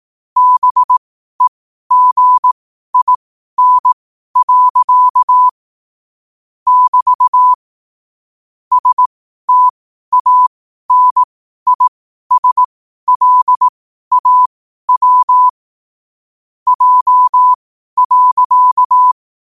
Morse code, heard as a keyed tone, about 18 wpm, spaced as for 12 wpm, 1000 Hz. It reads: BEGIN. <BT> STANISLAW J.
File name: qotdfiles / 12wpm